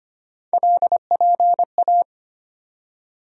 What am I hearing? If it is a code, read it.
Morse code, heard as a keyed tone, about 25 wpm, 700 Hz. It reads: LPA
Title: USA Call Sign Postfixes - 3 Letters 25wpm